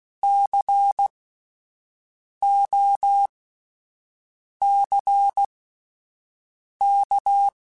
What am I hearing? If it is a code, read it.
Morse code, heard as a keyed tone, about 16 wpm, spaced as for 5 wpm, 800 Hz. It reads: COCK
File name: Alphabetical / Alphabetical mp3s / Cs @5